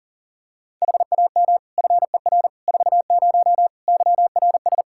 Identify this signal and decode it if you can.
Morse code, heard as a keyed tone, about 40 wpm, 700 Hz. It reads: HAM FER 40 YRS
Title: Practice QSO - Exchange 3 - More Info - 40wpm